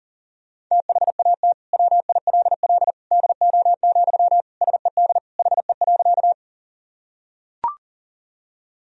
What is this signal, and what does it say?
Morse code, heard as a keyed tone, about 40 wpm, 700 Hz. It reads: THAT WILL DO, SED HE.
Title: Sentences from Top 100 Words and Abbreviations 40wpm